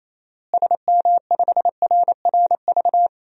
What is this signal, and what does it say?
Morse code, heard as a keyed tone, about 28 wpm, 700 Hz. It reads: SM5RRV
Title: DXCC-100 Call Signs - 6 Characters - Repeated 3x - 28wpm